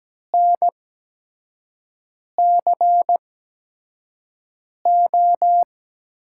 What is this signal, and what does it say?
Morse code, heard as a keyed tone, about 17 wpm, spaced as for 5 wpm, 700 Hz. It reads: NCO
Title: USA Call Signs - 5 Characters - 8x Character Spacing - 17wpm